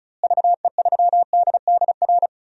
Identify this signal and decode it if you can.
Morse code, heard as a keyed tone, about 36 wpm, 700 Hz. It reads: VE3DDR
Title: DXCC-100 Call Signs - 6 Characters - Repeated 3x - 35wpm